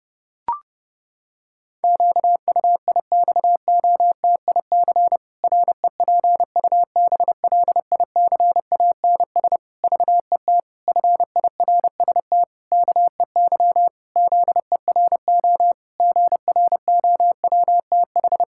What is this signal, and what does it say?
Morse code, heard as a keyed tone, about 30 wpm, 700 Hz. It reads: QUIXOTIC REPUBLICANS VET FIRST KEY ZERO GROWTH
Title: Pangram Challenge v2 - 30wpm